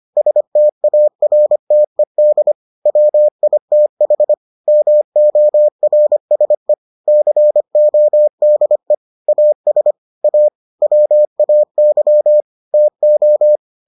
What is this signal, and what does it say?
Morse code, heard as a keyed tone, about 25 wpm, 600 Hz. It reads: STARTED WITH MORSE CODE AS A WAY TO